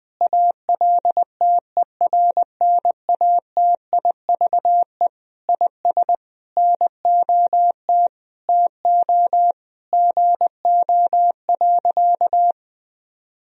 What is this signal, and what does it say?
Morse code, heard as a keyed tone, about 20 wpm, 700 Hz. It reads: ALTERNATIVE IS NOT TO GO.
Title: Sentences from Top 1000 Words - Review Part 4 - 20wpm